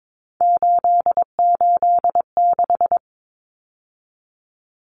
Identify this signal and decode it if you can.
Morse code, heard as a keyed tone, about 22 wpm, 700 Hz. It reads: 886
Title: Numbers - 3-Digit 22wpm